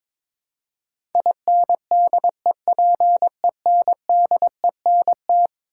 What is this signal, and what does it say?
Morse code, heard as a keyed tone, about 22 wpm, 700 Hz. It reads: INDEPENDENT
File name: practice / 22wpm / Eleven Letter Words - 22wpm